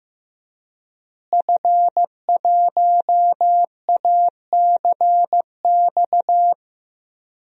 Morse code, heard as a keyed tone, about 15 wpm, 700 Hz. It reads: F1ACX
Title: DXCC-100 Call Signs - 5 Characters - Repeated 3x - 15wpm